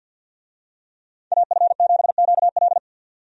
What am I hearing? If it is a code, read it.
Morse code, heard as a keyed tone, about 45 wpm, 700 Hz. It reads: AF7QL
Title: USA Call Signs - 5 Characters - 50wpm